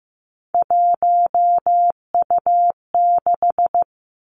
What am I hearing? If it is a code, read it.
Morse code, heard as a keyed tone, about 15 wpm, 700 Hz. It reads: 1U6
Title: Three Challenging Letter-Number Combinations 15wpm